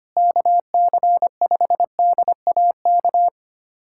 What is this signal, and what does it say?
Morse code, heard as a keyed tone, about 25 wpm, 700 Hz. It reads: KC5DAK
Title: USA Callsign DE Callsign 25wpm